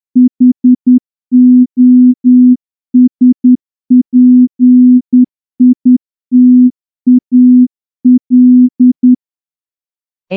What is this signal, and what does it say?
Morse code, heard as a keyed tone, about 10 wpm, 250 Hz. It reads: HOSPITAL
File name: morse-training-10WPM